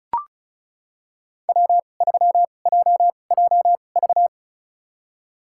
Morse code, heard as a keyed tone, about 36 wpm, spaced as for 25 wpm, 700 Hz. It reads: W3JJV
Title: USA Call Signs - 5 Characters - 2x Character Spacing - 35wpm